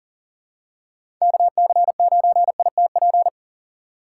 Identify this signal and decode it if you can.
Morse code, heard as a keyed tone, about 40 wpm, 700 Hz. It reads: KC9ITP
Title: USA Call Signs - 6x Character Spacing - 40wpm